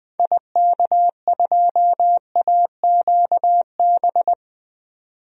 Morse code, heard as a keyed tone, about 20 wpm, 700 Hz. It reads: IK2AQB
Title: DXCC-100 Call Signs - 6 Characters - 20wpm